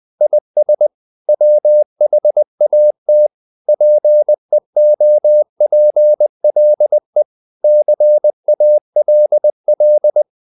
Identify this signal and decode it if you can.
Morse code, heard as a keyed tone, about 20 wpm, 600 Hz. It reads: IS WHAT PEOPLE CALL
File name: the old man and the sea / the old man and the sea part 1 20wpm